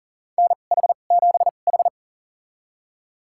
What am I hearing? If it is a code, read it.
Morse code, heard as a keyed tone, about 40 wpm, spaced as for 29 wpm, 700 Hz. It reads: NH7H
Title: USA Call Signs - 4 Characters - 2x Character Spacing - 40wpm